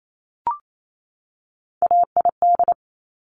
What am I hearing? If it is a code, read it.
Morse code, heard as a keyed tone, about 28 wpm, 700 Hz. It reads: AID
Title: Top Three Letter Combinations in Words 28wpm